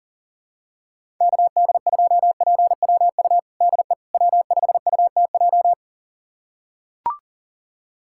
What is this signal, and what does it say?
Morse code, heard as a keyed tone, about 40 wpm, 700 Hz. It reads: KD2PWU DE W5UTJ
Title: USA Callsign DE Callsign 40wpm